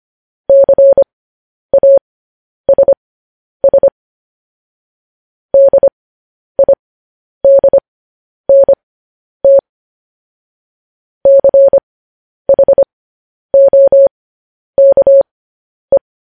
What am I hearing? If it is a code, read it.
Morse code, heard as a keyed tone, about 25 wpm, spaced as for 10 wpm, 550 Hz. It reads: CASS DIDNT CHOKE